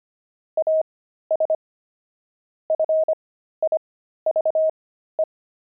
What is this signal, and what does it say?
Morse code, heard as a keyed tone, about 25 wpm, spaced as for 13 wpm, 630 Hz. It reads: AS FIVE
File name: pangram-fs-13-cs-25-steaks